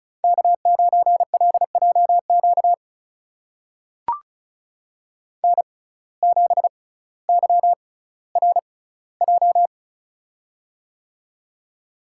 Morse code, heard as a keyed tone, about 34 wpm, spaced as for 18 wpm, 700 Hz. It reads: K9LJQ N7YRJ
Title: USA Call Signs - 5 Characters - 6x Character Spacing - 35wpm